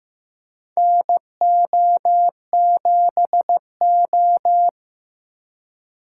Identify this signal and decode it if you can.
Morse code, heard as a keyed tone, about 15 wpm, 700 Hz. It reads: NO7O
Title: USA Call Signs - 4 Characters - 3x Character Spacing - 15wpm